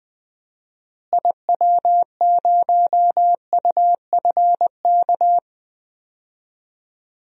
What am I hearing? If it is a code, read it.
Morse code, heard as a keyed tone, about 20 wpm, 700 Hz. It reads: IW0UFK
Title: DXCC-100 Call Signs - 6 Characters - Repeated 2x - 20wpm